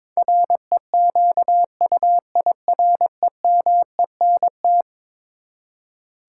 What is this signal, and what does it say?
Morse code, heard as a keyed tone, about 22 wpm, 700 Hz. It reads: REQUIREMENT
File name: practice / 22wpm / Eleven Letter Words - 22wpm